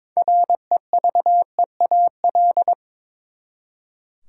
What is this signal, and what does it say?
Morse code, heard as a keyed tone, about 21 wpm, 700 Hz. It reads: REVEAL
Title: Six Letter Words - 22wpm